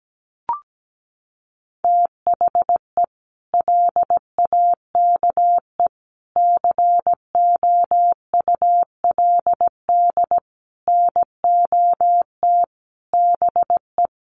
Morse code, heard as a keyed tone, about 17 wpm, 700 Hz. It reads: THE LAKE COULD NOT BE
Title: Sentences from Top 1100 Words 17wpm